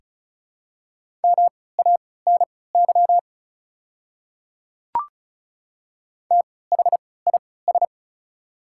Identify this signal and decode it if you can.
Morse code, heard as a keyed tone, about 36 wpm, spaced as for 20 wpm, 690 Hz. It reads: MANY THIS
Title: 3x Character Spacing - Top 100 Words - 35wpm